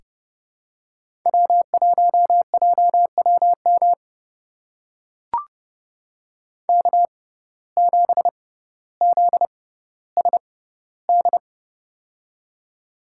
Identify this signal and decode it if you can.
Morse code, heard as a keyed tone, about 30 wpm, spaced as for 15 wpm, 700 Hz. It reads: W1JWM K7ZSD
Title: USA Call Signs - 6x Character Spacing - 30wpm